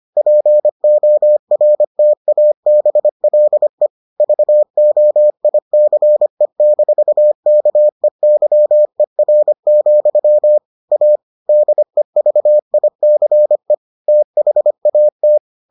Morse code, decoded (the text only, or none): PORTABLE VOICE-KEYER, A DEVICE THAT